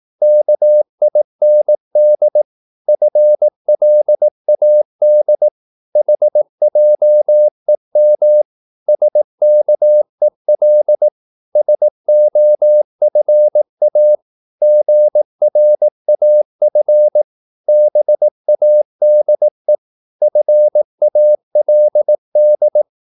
Morse code, heard as a keyed tone, about 18 wpm, 600 Hz. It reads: KIND FLAD HJEM SKEL SOFA GRAF BADE FALD
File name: Korte ord DK 18wpm